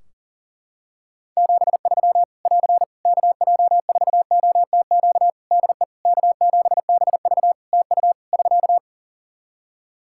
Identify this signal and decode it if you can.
Morse code, heard as a keyed tone, about 40 wpm, 700 Hz. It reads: 73 <AR> KJ4OTQ DE K7BV TU <SK>
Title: K1USN SST QSO Practice 40wpm